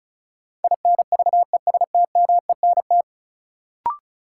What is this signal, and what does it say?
Morse code, heard as a keyed tone, about 36 wpm, 700 Hz. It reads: INVESTMENT
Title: Sentences from Top 700 Words - New Word Drill - 35wpm